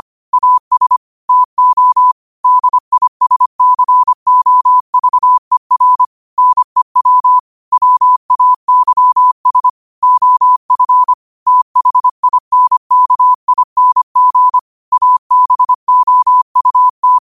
Morse code, heard as a keyed tone, about 25 wpm, 1000 Hz. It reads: AS TO DISCOVER NEW WAYS OF THINKING ABOUT